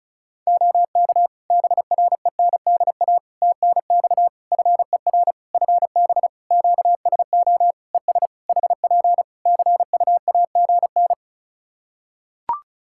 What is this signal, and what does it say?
Morse code, heard as a keyed tone, about 34 wpm, 700 Hz. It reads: OK BRENDA TNX FER FB QSO ES HP CUAGN
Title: Practice QSO - Exchange 4 - Ending - 35wpm